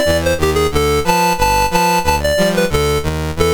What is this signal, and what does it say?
There is no code.